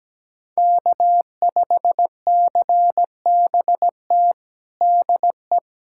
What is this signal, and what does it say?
Morse code, heard as a keyed tone, about 17 wpm, 700 Hz. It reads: K5CBT DE
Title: USA Callsign DE Callsign - 5 Characters - 17wpm